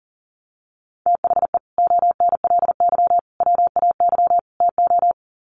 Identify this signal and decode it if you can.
Morse code, heard as a keyed tone, about 40 wpm, 700 Hz. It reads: THE ONLY WAY TO
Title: Sets of 4 Words 40wpm